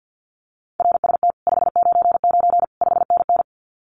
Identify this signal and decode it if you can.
Morse code, heard as a keyed tone, about 50 wpm, 700 Hz. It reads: RST 599 5NN
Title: Common QSO Phrases Encore 50wpm